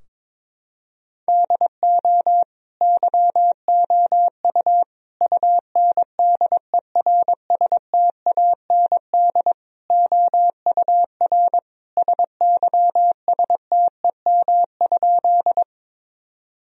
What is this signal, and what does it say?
Morse code, heard as a keyed tone, about 21 wpm, 700 Hz. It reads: DO YOU UNDERSTAND OUR SYSTEM?
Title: Sentences from Top 1000 Words - Review Part 3 - 22wpm